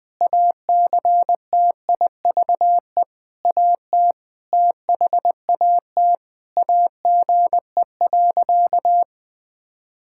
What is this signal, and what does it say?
Morse code, heard as a keyed tone, about 20 wpm, 700 Hz. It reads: ACTIVE AT THAT AGE.
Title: Sentences from Top 1000 Words - Review Part 6 - 20wpm